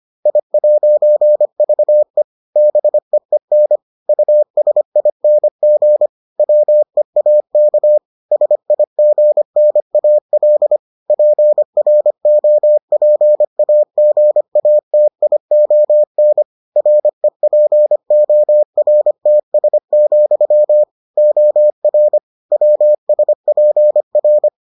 Morse code, heard as a keyed tone, about 25 wpm, 600 Hz. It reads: I'VE BEEN USING WEAK SIGNAL PROPAGATION REPORTS, OR WSPR